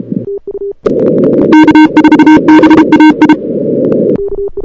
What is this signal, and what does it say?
Morse code, heard as a keyed tone, about 31 wpm, 930 Hz, with noise. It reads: K4BAI
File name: CWSprint / Feb2006 / 2006-feb-cw-sprint-k5tr op-n5rz -02:00